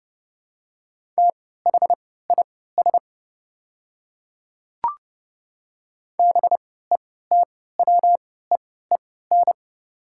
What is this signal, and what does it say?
Morse code, heard as a keyed tone, about 30 wpm, spaced as for 17 wpm, 700 Hz. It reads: THIS BETWEEN
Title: 3x Character Spacing - Top 300 Words - 30wpm